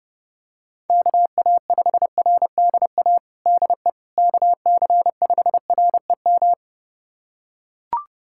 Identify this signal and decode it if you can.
Morse code, heard as a keyed tone, about 30 wpm, 700 Hz. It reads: KA5RDA DE KC5REM
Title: USA Callsign DE Callsign 30wpm